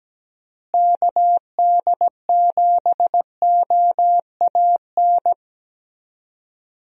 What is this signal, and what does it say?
Morse code, heard as a keyed tone, about 17 wpm, 700 Hz. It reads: KD7OAN
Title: USA Call Signs - 6 Characters - Repeated 3x - 17wpm